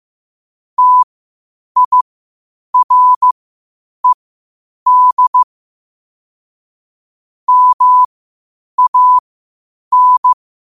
Morse code, heard as a keyed tone, about 15 wpm, spaced as for 8 wpm, 1000 Hz. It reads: TIRED MAN
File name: 8wpm